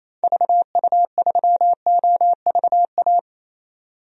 Morse code, heard as a keyed tone, about 28 wpm, 700 Hz. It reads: VU3OVA
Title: DXCC-100 Call Signs - 6 Characters - 28wpm